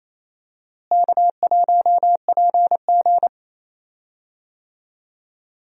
Morse code, heard as a keyed tone, about 28 wpm, 700 Hz. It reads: K1PG
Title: USA Call Signs - 4 Characters - Repeated 3x - 28wpm